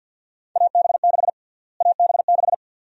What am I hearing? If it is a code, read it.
Morse code, heard as a keyed tone, about 53 wpm, 700 Hz. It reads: AB6 AB6
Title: USA Call Sign Prefixes - Repeated 2x - 50wpm